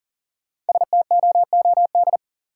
Morse code, heard as a keyed tone, about 40 wpm, 700 Hz. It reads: STOOD
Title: Sentences from Top 500 Words - New Word Drill - 40wpm